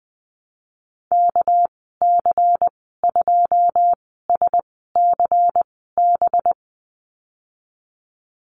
Morse code, heard as a keyed tone, about 20 wpm, spaced as for 15 wpm, 700 Hz. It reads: KC2SCB
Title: 238 Lesson 38g - Letter q - Call Signs 20wpm